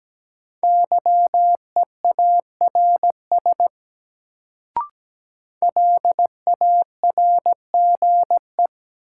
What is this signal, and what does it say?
Morse code, heard as a keyed tone, about 17 wpm, 700 Hz. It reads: YEARS LARGE
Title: Sentences from Top 200 Words - New Word Drill - 17wpm